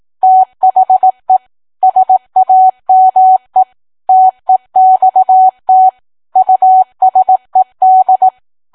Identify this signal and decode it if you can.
Morse code, heard as a keyed tone, about 18 wpm, 750 Hz. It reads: THE SAME TEXT USED